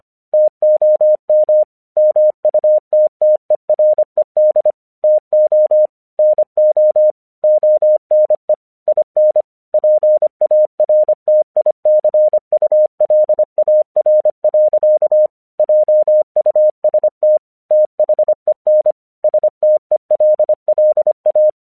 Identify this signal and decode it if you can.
Morse code, heard as a keyed tone, about 25 wpm, 610 Hz. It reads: TOM MUTTERED TO NO ONE IN PARTICULAR. JUST THEN STELLA
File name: 1 English 25wpm / AQC02